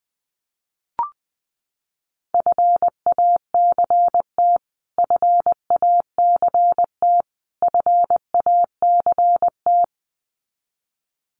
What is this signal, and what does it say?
Morse code, heard as a keyed tone, about 20 wpm, 700 Hz. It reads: FACT FACT FACT
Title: Top 500 Words Repeated 20wpm